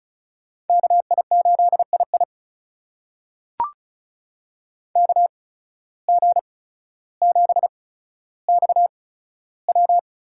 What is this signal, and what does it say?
Morse code, heard as a keyed tone, about 36 wpm, spaced as for 15 wpm, 700 Hz. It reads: KI8II KG7XW
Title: USA Call Signs - 5 Characters - 8x Character Spacing - 35wpm